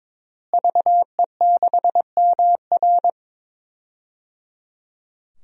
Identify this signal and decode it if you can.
Morse code, heard as a keyed tone, about 22 wpm, 700 Hz. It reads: VE6MR
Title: DXCC-100 Call Signs - 5 Characters - Repeated 3x - 22wpm